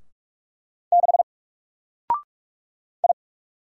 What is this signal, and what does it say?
Morse code, heard as a keyed tone, about 44 wpm, 700 Hz. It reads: 6 I
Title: Single Letter-Number 45wpm